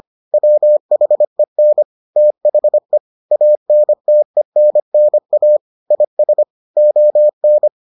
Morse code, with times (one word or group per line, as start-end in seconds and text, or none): 0.3-1.8 WHEN
2.2-3.0 THE
3.3-5.6 ANTENNA
5.9-6.4 IS
6.8-7.7 ON